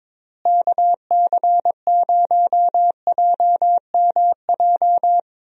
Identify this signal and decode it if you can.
Morse code, heard as a keyed tone, about 22 wpm, 700 Hz. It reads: KC0JMJ
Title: USA Call Signs - 6 Characters - Repeated 3x - 22wpm